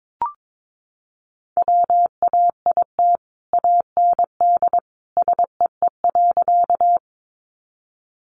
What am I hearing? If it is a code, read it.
Morse code, heard as a keyed tone, about 22 wpm, 700 Hz. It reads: WAIT AND SEE.
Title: Sentences from Top 500 Words 22wpm